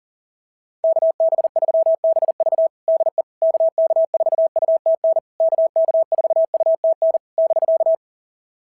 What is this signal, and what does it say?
Morse code, heard as a keyed tone, about 37 wpm, 660 Hz. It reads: KB3BV DE KK4UTN KK4UTN <BK>
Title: SOTA QSO Practice 40wpm